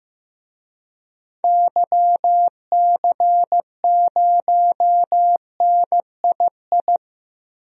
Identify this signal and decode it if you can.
Morse code, heard as a keyed tone, about 15 wpm, 700 Hz. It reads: YC0NII
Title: DXCC-100 Call Signs - 6 Characters - Repeated 2x - 15wpm